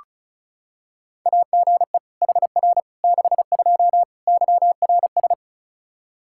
Morse code, heard as a keyed tone, about 36 wpm, 700 Hz. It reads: AGE HR 62 YRS
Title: Practice QSO - Exchange 3 - More Info - 35wpm